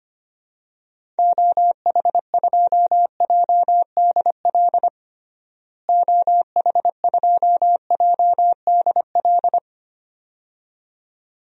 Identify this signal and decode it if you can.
Morse code, heard as a keyed tone, about 25 wpm, 700 Hz. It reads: OH2JDL OH2JDL
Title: DXCC-100 Call Signs - 6 Characters - Repeated 2x - 25wpm